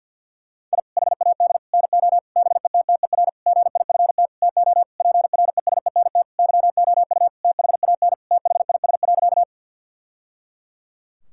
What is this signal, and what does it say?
Morse code, heard as a keyed tone, about 50 wpm, 700 Hz. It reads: I HAD NO BETTER GIFT TO PRESENT YOU THAN THIS.
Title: Sentences from Top 1000 Words - Review Part 6 - 50wpm